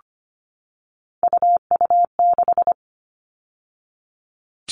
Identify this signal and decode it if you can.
Morse code, heard as a keyed tone, about 25 wpm, 700 Hz. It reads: UU6
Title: All Cut Numbers - 3-Digit 25wpm